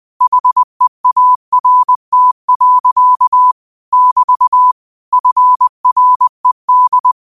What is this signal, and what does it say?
Morse code, heard as a keyed tone, about 20 wpm, 1000 Hz. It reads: HEART. <BT> FRED